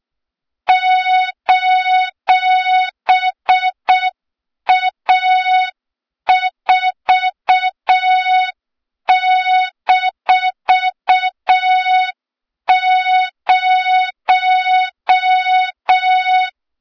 Morse code, heard as a keyed tone, about 6 wpm, 3700 Hz, with noise. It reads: :A4-0
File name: Morze-HA2MN / hang / 23 GYAKORLAT